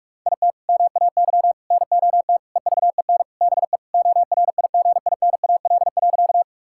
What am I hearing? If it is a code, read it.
Morse code, heard as a keyed tone, about 46 wpm, 700 Hz. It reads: IT MAY NOT EVEN BE ORIGINAL.
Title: Sentences from Top 1000 Words - Review Part 6 - 45wpm